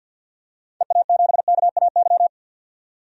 Morse code, heard as a keyed tone, about 50 wpm, 700 Hz. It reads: EA7KAY
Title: DXCC-100 Call Signs - 6 Characters - 50wpm